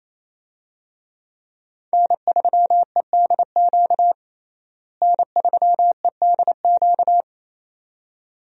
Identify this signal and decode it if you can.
Morse code, heard as a keyed tone, about 27 wpm, 700 Hz. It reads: N3EDQ N3EDQ
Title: USA Call Signs - 5 Characters - Repeated 2x - 28wpm